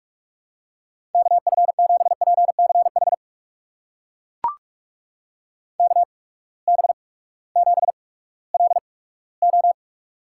Morse code, heard as a keyed tone, about 42 wpm, spaced as for 20 wpm, 700 Hz. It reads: KF7PCH KB7LO
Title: USA Call Signs - 6 Characters - 8x Character Spacing - 45wpm